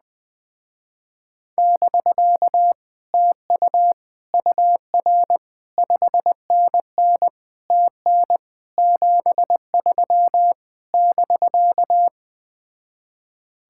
Morse code, heard as a keyed tone, about 20 wpm, 700 Hz. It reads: <BK> TU UR 5NN TN 73 <BK>
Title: Speed-Racing Common QSO Phrases - 17wpm